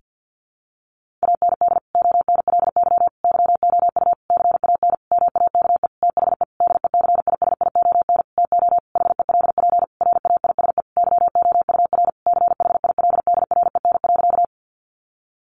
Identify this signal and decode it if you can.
Morse code, heard as a keyed tone, about 50 wpm, 700 Hz. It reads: AND ONLY YOU CAN MAKE THE DECISION TO HELP RAISE YOUR CHILDREN.
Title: Sentences from Top 1400 Words 50wpm